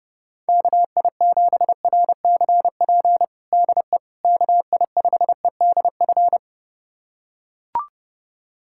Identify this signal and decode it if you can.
Morse code, heard as a keyed tone, about 32 wpm, 700 Hz. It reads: KI7RCP DE KI5EDF
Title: USA Callsign DE Callsign 30wpm